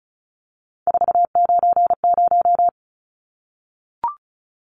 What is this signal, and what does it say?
Morse code, heard as a keyed tone, about 34 wpm, 700 Hz. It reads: 490
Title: Numbers - 3-Digit 35wpm